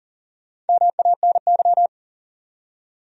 Morse code, heard as a keyed tone, about 59 wpm, spaced as for 49 wpm, 700 Hz. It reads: MANY
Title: Speed-Racing Top 500 Words 40wpm